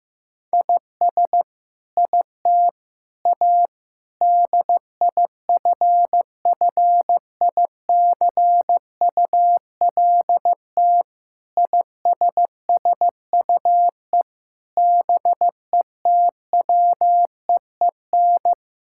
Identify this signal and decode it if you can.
Morse code, heard as a keyed tone, about 15 wpm, 700 Hz. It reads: IS IT A DIFFICULT ISSUE BETWEEN
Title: Sentences from Top 1000 Words - Review Part 6 - 15wpm